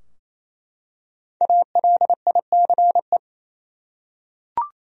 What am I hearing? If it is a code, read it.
Morse code, heard as a keyed tone, about 28 wpm, 700 Hz. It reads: ALICE
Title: USA Common Names - 28wpm